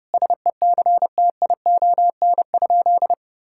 Morse code, heard as a keyed tone, about 30 wpm, 700 Hz. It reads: SECTION?
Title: Sentences from Top 1000 Words - Review Part 5 - 30wpm